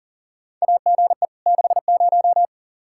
Morse code, heard as a keyed tone, about 40 wpm, 700 Hz. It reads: AGE 60
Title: Practice QSO - Exchange 3 - More Info - 40wpm